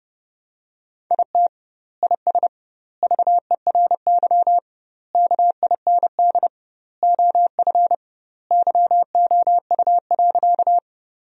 Morse code, heard as a keyed tone, about 30 wpm, 700 Hz. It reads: IT IS VERY KIND OF YOU.
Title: 2x Word-Spacing - Sentences from Top 200 Words - 30wpm